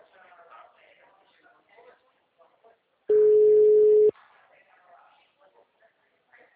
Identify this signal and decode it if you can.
Ringback tone (Europe, 425 Hz), with speech.